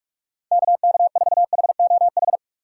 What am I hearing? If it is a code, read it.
Morse code, heard as a keyed tone, about 41 wpm, 700 Hz. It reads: KK4HOH
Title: USA Call Signs - 6 Characters - 2x Character Spacing - 45wpm